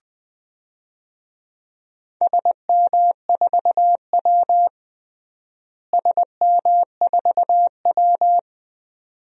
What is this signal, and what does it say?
Morse code, heard as a keyed tone, about 20 wpm, 700 Hz. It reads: SM4W SM4W
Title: DXCC-100 Call Signs - 4 Characters - Repeated 2x - 20wpm